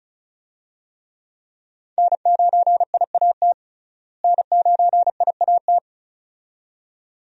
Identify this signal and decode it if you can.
Morse code, heard as a keyed tone, about 36 wpm, 700 Hz. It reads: N9IAT N9IAT
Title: USA Call Signs - 5 Characters - Repeated 2x - 35wpm